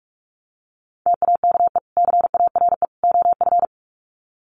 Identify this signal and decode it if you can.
Morse code, heard as a keyed tone, about 44 wpm, 700 Hz. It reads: TAKE CARE OF
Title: Sets of 3 Words 45wpm